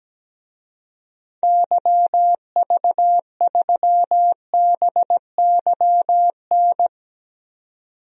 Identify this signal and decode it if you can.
Morse code, heard as a keyed tone, about 17 wpm, 700 Hz. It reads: YV3BYN
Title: DXCC-100 Call Signs - 6 Characters - 17wpm